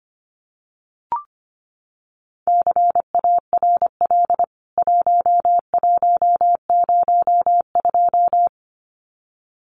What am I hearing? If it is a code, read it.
Morse code, heard as a keyed tone, about 25 wpm, 700 Hz. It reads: CARL 1102
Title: CWT Exchange 25wpm